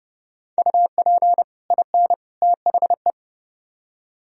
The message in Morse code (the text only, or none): UP IN THE